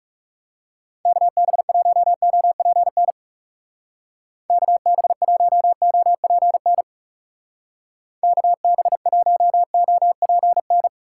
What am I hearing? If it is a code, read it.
Morse code, heard as a keyed tone, about 35 wpm, 700 Hz. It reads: KB1OPN KB1OPN KB1OPN
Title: Speed Racing - USA 2x3 Call Signs 30wpm